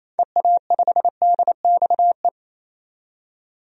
Morse code, heard as a keyed tone, about 28 wpm, 700 Hz. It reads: EA5DXE
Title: DXCC-100 Call Signs - 6 Characters - 28wpm